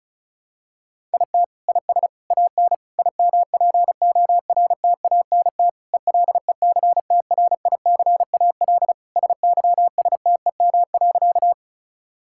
Morse code, heard as a keyed tone, about 36 wpm, 700 Hz. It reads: IT IS AN IMPORTANT ELECTRICAL SYSTEM.
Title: Sentences from Top 1000 Words - Review Part 6 - 35wpm